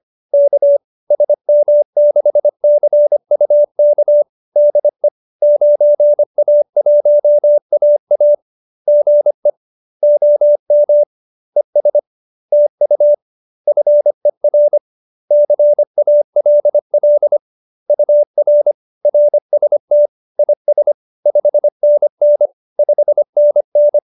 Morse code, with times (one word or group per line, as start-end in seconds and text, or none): 0.3-0.8 K
1.1-4.2 SM6CUK
4.6-5.1 DE
5.4-8.3 9A1AA
8.9-9.5 GE
10.0-11.0 OM
11.6-12.0 ES
12.5-13.1 TU
13.7-14.8 FER
15.3-17.4 CALL
17.9-18.7 UR
19.1-20.1 RST
20.4-20.9 IS
21.3-22.5 5NN
22.8-24.0 5NN